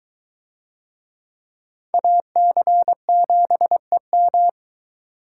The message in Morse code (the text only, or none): AC7EM